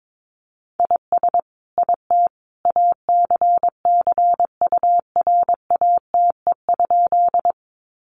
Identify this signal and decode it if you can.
Morse code, heard as a keyed tone, about 22 wpm, 700 Hz. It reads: IS IT ACCURATE?
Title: Sentences from Top 1200 Words 22wpm